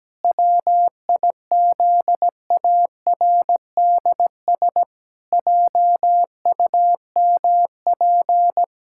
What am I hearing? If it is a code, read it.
Morse code, heard as a keyed tone, about 17 wpm, 700 Hz. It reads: WIZARDS JUMP
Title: Pangram Challenge v2 - 17wpm